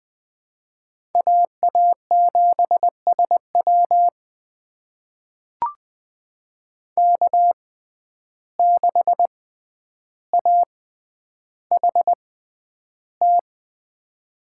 Morse code, heard as a keyed tone, about 20 wpm, spaced as for 10 wpm, 700 Hz. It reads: AA7SW K6AHT
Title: USA Call Signs - 5 Characters - 6x Character Spacing - 20wpm